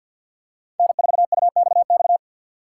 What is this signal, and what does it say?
Morse code, heard as a keyed tone, about 50 wpm, 700 Hz. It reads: N4UXX
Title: USA Call Signs - 5 Characters - 6x Character Spacing - 50wpm